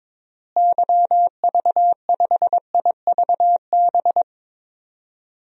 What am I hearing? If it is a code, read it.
Morse code, heard as a keyed tone, about 21 wpm, 700 Hz. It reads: YV5IVB
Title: DXCC-100 Call Signs - 6 Characters - Repeated 3x - 22wpm